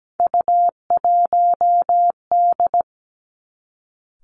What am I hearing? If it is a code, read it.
Morse code, heard as a keyed tone, about 17 wpm, 700 Hz. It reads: U1D
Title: Three Challenging Letter-Number Combinations 17wpm